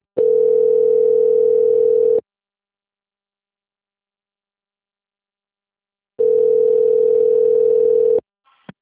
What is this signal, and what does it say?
Ringback tone (North America), 440 + 480 Hz.
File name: out- - - - - .35257